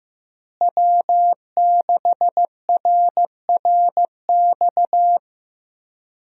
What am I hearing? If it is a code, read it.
Morse code, heard as a keyed tone, about 15 wpm, 700 Hz. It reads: W6RRX
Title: USA Call Signs - 5 Characters - Repeated 2x - 15wpm